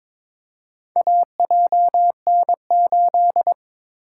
Morse code, heard as a keyed tone, about 23 wpm, 700 Hz. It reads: AJN8
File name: practice / 22wpm / Four Letter-Number Combinations - 22wpm